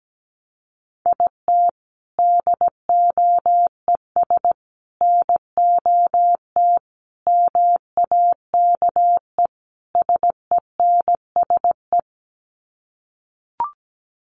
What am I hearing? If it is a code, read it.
Morse code, heard as a keyed tone, about 17 wpm, 700 Hz. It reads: IT DOES NOT MAKE SENSE
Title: Sets of 5 Words Encore2 17wpm